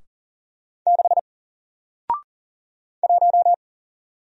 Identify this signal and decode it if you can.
Morse code, heard as a keyed tone, about 40 wpm, 700 Hz. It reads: 6 1
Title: Single Character (Letter:Number:Punctuation:Prosign) Alt 40wpm